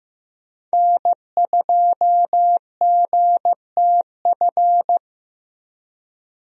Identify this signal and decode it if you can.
Morse code, heard as a keyed tone, about 15 wpm, 700 Hz. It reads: N2GTF
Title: USA Call Signs - 5 Characters - 15wpm